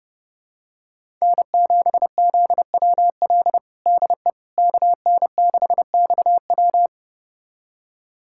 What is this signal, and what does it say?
Morse code, heard as a keyed tone, about 30 wpm, 700 Hz. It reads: N7ZWL DE KN6XW
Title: USA Callsign DE Callsign - 5 Characters - 30wpm